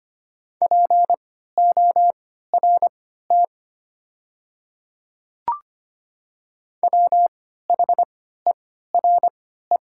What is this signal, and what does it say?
Morse code, heard as a keyed tone, about 25 wpm, spaced as for 14 wpm, 700 Hz. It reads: PORT WHERE
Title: 3x Character Spacing - Top 200 Words - 25wpm